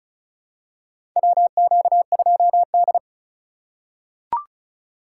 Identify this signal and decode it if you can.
Morse code, heard as a keyed tone, about 36 wpm, 700 Hz. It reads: WQ2D
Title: USA Call Signs - 4 Characters - 4x Character Spacing - 35wpm